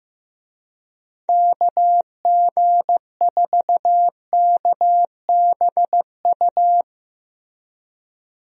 Morse code, heard as a keyed tone, about 15 wpm, 700 Hz. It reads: KG4KBU
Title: USA Call Signs - 6 Characters - Repeated 2x - 15wpm